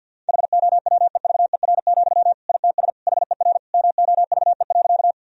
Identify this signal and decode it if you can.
Morse code, heard as a keyed tone, about 50 wpm, 700 Hz. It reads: HOWEVER, ITS HER MOVE.
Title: Sentences from Top 1000 Words - Review Part 1 - 50wpm